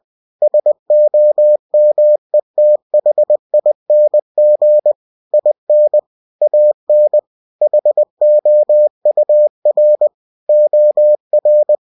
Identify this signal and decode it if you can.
Morse code, heard as a keyed tone, about 20 wpm, 600 Hz. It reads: SOMETHING IN AN HOUR OR
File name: cw-books / winnie-the-pooh / winnie-20wpm-page-0076